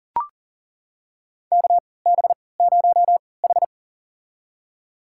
Morse code, heard as a keyed tone, about 40 wpm, spaced as for 23 wpm, 700 Hz. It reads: KB0H
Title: USA Call Signs - 4 Characters - 3x Character Spacing - 40wpm